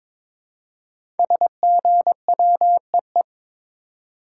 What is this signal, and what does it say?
Morse code, heard as a keyed tone, about 22 wpm, 700 Hz. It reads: SGWEE
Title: Code Groups - 5 Letters - 22wpm